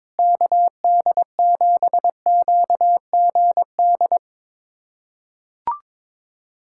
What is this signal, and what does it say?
Morse code, heard as a keyed tone, about 22 wpm, 700 Hz. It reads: KD7QGD
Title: USA Callsign DE Callsign - 6 Characters - 22wpm